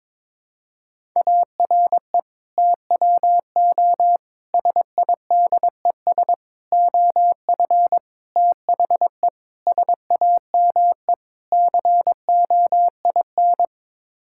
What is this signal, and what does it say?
Morse code, heard as a keyed tone, about 22 wpm, 700 Hz. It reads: ARE TWO SIDES OF THE SAME COIN
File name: practice / 22wpm / Sets of 7 Words - 22wpm